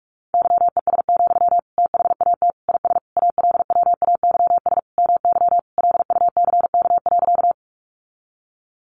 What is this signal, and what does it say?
Morse code, heard as a keyed tone, about 45 wpm, 700 Hz. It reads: YES, THAT IS ALWAYS MY LUCK.
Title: Sentences from Top 1500 Words 45wpm